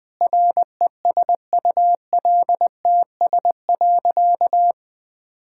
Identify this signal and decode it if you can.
Morse code, heard as a keyed tone, about 20 wpm, 700 Hz. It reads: RESULTS.
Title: Sentences with 7-Word Phrases - 20wpm